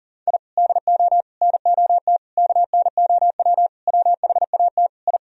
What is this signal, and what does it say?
Morse code, heard as a keyed tone, about 40 wpm, 700 Hz. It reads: I DO NOT KNOW WHAT I
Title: Sentences from Top 1000 Words - Review Part 3 - 40wpm